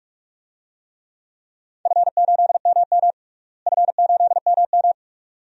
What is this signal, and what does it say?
Morse code, heard as a keyed tone, about 44 wpm, 700 Hz. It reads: F8MM F8MM
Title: DXCC-100 Call Signs - 4 Characters - Repeated 2x - 45wpm